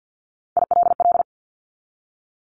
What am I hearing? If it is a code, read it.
Morse code, heard as a keyed tone, about 50 wpm, 700 Hz. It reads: ILL
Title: Top 1751-2000 Words 50wpm